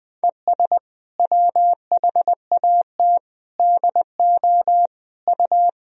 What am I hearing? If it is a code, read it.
Morse code, heard as a keyed tone, about 20 wpm, 700 Hz. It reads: ES WHAT DO U
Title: Sentences from Top 100 Words and Abbreviations 20wpm